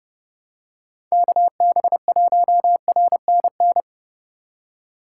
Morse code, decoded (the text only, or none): KB1RNN